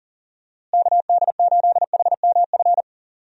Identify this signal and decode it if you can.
Morse code, heard as a keyed tone, about 40 wpm, 700 Hz. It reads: KD8HMF
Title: USA Call Signs - 6 Characters - Repeated 2x - 40wpm